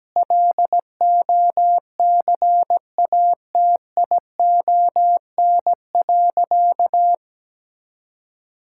Morse code, heard as a keyed tone, about 17 wpm, 700 Hz. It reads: LOCATION.
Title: Sentences from Top 1000 Words - Review Part 6 - 17wpm